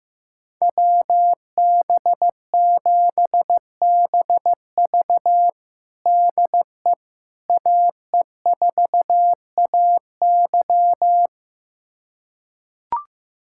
Morse code, heard as a keyed tone, about 15 wpm, 700 Hz. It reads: WB7BV DE AE4AY
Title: USA Callsign DE Callsign - 5 Characters - 15wpm